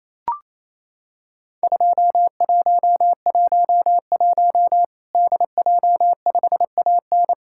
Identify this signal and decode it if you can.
Morse code, heard as a keyed tone, about 27 wpm, 700 Hz. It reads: 2111 DJ5AN
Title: Worked All Europe DX Contest - QTC Exchange - 28wpm